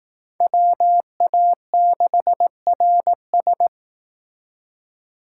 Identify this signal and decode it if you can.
Morse code, heard as a keyed tone, about 19 wpm, 700 Hz. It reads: WA6RS
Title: Speed Racing - USA 2x2 Call Signs 15wpm